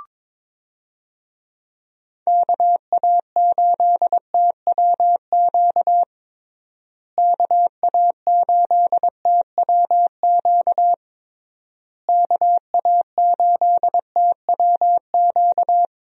Morse code, heard as a keyed tone, about 22 wpm, 700 Hz. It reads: KA8TWQ KA8TWQ KA8TWQ
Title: USA Call Signs - 6 Characters - Repeated 3x - 22wpm